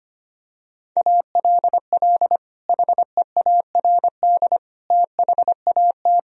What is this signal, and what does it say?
Morse code, heard as a keyed tone, about 25 wpm, 700 Hz. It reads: ALL HEARD THAT
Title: Sentences from Top 1000 Words - Review Part 1 - 25wpm